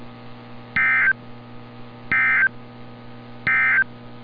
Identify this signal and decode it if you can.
Emergency Alert System (SAME) tones, heard as harsh buzzing bursts. It is heard from 0.6 to 3.9 s.
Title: NOAA Weather Radio - Altoona, PA